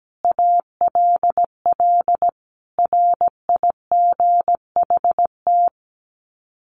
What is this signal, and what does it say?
Morse code, heard as a keyed tone, about 17 wpm, 700 Hz. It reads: ALL RIGHT
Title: Sets of 5 Words Encore2 17wpm